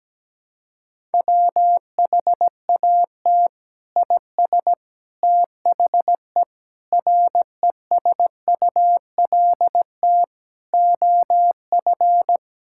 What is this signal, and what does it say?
Morse code, heard as a keyed tone, about 17 wpm, 700 Hz. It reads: WHAT IS THE RESULT OF
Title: Sentences from Top 1000 Words - Review Part 5 - 17wpm